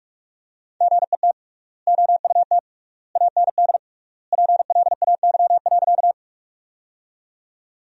Morse code, heard as a keyed tone, about 46 wpm, 700 Hz. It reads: GET OUT AND PLAY.
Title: 3x Word-Spacing - Sentences from Top 200 Words - 45wpm